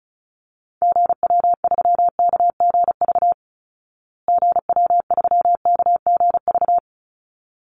Morse code, heard as a keyed tone, about 34 wpm, 700 Hz. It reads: GW3KGV GW3KGV
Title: Call Signs Repeated 3x - 35wpm